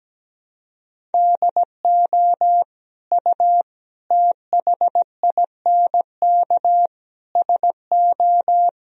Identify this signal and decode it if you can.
Morse code, heard as a keyed tone, about 17 wpm, 700 Hz. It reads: DO U THINK SO
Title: Sentences from Top 200 Words and Abbreviations 17wpm